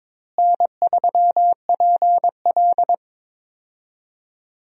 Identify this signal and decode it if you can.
Morse code, heard as a keyed tone, about 22 wpm, 700 Hz. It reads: N3PL
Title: USA Call Signs - 4 Characters - Repeated 3x - 22wpm